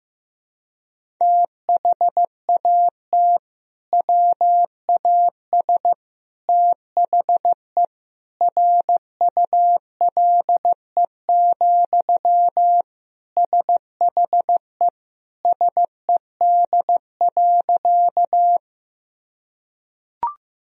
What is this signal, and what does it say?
Morse code, heard as a keyed tone, about 15 wpm, 700 Hz. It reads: THAT WAS THE RULE, SHE SED.
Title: Sentences from Top 400 Words and Abbreviations 15wpm